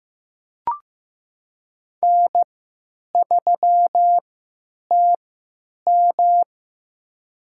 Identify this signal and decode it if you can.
Morse code, heard as a keyed tone, about 15 wpm, spaced as for 9 wpm, 700 Hz. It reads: N3TM
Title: USA Call Signs - 4 Characters - 3x Character Spacing - 15wpm